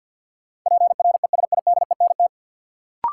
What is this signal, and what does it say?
Morse code, heard as a keyed tone, about 53 wpm, 700 Hz. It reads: PRESIDENT
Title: Sentences from Top 1000 Words - New Word Drill - 50wpm